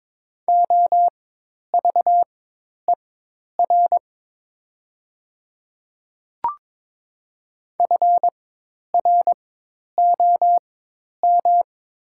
Morse code, heard as a keyed tone, about 22 wpm, spaced as for 10 wpm, 700 Hz. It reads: OVER FROM
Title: 4x Character Spacing - Top 500 Words - 22wpm